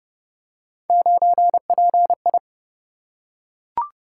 Morse code, heard as a keyed tone, about 28 wpm, 700 Hz. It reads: 9PI